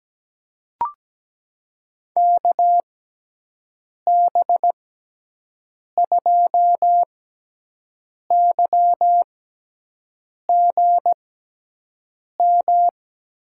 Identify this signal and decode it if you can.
Morse code, heard as a keyed tone, about 17 wpm, spaced as for 6 wpm, 700 Hz. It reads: KB2YGM
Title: USA Call Signs - 6 Characters - 6x Character Spacing - 17wpm